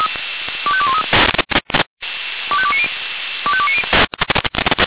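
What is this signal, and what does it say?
Five-tone selective call (ZVEI), heard as quick runs of five tones, tones of about 70 ms. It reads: 25213 25290 25290
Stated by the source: ZVEI five-tone selective call received on an SDR.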